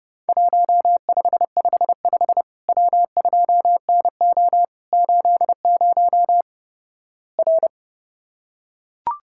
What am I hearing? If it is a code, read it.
Morse code, heard as a keyed tone, about 29 wpm, 700 Hz. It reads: 1555 W2NO 80 R
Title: Worked All Europe DX Contest - QTC Exchange - 30wpm